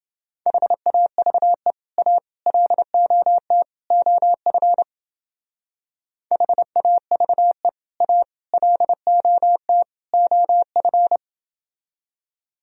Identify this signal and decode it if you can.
Morse code, heard as a keyed tone, about 28 wpm, 700 Hz. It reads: HAVE A LOT OF HAVE A LOT OF